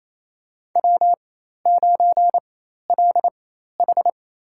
Morse code, heard as a keyed tone, about 28 wpm, spaced as for 13 wpm, 700 Hz. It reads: W9LH
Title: USA Call Signs - 4x Character Spacing - 28wpm